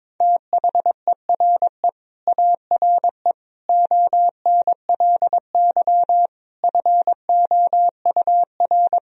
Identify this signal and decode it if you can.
Morse code, heard as a keyed tone, about 22 wpm, 700 Hz. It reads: THERE ARE ONLY FOUR
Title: Sentences from Top 1000 Words - Review Part 1 - 22wpm